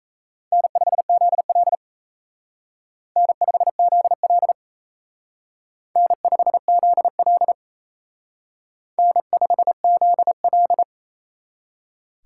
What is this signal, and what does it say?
Morse code, heard as a keyed tone, about 35 wpm, 700 Hz. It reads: N5ZL N5ZL N5ZL N5ZL